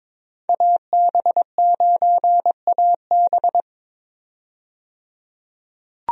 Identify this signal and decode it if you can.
Morse code, heard as a keyed tone, about 22 wpm, 700 Hz. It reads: AB9AB
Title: USA Call Signs - 5 Characters - 22wpm